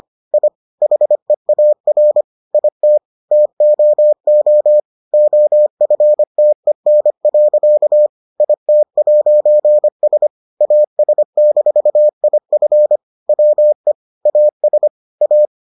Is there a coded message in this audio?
Yes: Morse code from 0.3 to 15.5 s, heard as a keyed tone, about 25 wpm, 600 Hz. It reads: I HEAR IT TOO OFTEN. IT'S AS-IF WE AS A